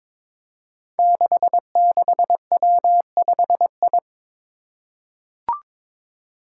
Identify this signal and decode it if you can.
Morse code, heard as a keyed tone, about 22 wpm, 700 Hz. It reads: 66W5I
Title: Five Challenging Letter-Number Combinations 22wpm